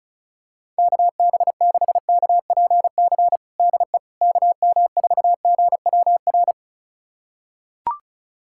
Morse code, heard as a keyed tone, about 36 wpm, 700 Hz. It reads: KB6KPC DE KM4GWR
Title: USA Callsign DE Callsign - 6 Characters - 35wpm